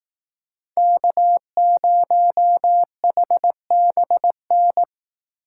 Morse code, heard as a keyed tone, about 18 wpm, 700 Hz. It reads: K0HBN
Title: Speed Racing - USA 1x3 Call Signs 15wpm